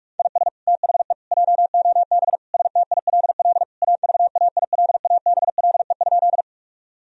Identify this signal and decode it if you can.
Morse code, heard as a keyed tone, about 41 wpm, 700 Hz. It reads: IS THE JOB STILL AVAILABLE?
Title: Sentences from Top 1000 Words - Review Part 3 - 45wpm